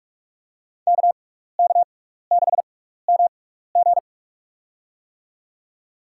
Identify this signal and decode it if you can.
Morse code, heard as a keyed tone, about 48 wpm, spaced as for 16 wpm, 700 Hz. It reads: KK6MG
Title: USA Call Signs - 5 Characters - 6x Character Spacing - 45wpm